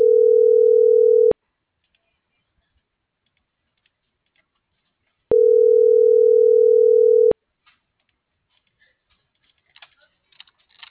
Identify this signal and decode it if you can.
Ringback tone (North America), 440 + 480 Hz.